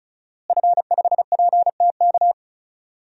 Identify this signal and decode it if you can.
Morse code, heard as a keyed tone, about 36 wpm, 700 Hz. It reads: F5PTK
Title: DXCC-100 Call Signs - 5 Characters - Repeated 2x - 35wpm